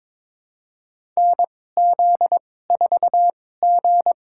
Morse code, heard as a keyed tone, about 22 wpm, spaced as for 16 wpm, 700 Hz. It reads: NZ4G
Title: USA Call Signs - 2x Character Spacing - 22wpm